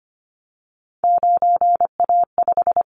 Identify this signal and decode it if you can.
Morse code, heard as a keyed tone, about 25 wpm, 700 Hz. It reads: 9A5
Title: Callsign Prefixes for Top 50 DXCC - 25wpm